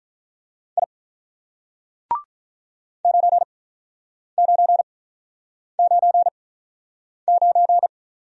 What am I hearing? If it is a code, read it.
Morse code, heard as a keyed tone, about 46 wpm, spaced as for 9 wpm, 700 Hz. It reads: I 9999